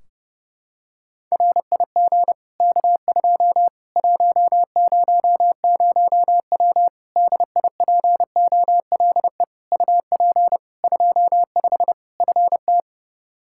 Morse code, heard as a keyed tone, about 31 wpm, 700 Hz. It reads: RIG K2 100W DIPOLE UP 25 FT
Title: Practice QSO - Exchange 2 - Info - 30wpm